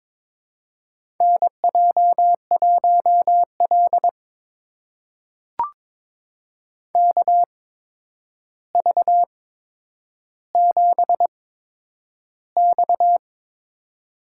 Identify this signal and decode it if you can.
Morse code, heard as a keyed tone, about 23 wpm, spaced as for 10 wpm, 700 Hz. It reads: NJ1L KV7X
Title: USA Call Signs - 4 Characters - 8x Character Spacing - 22wpm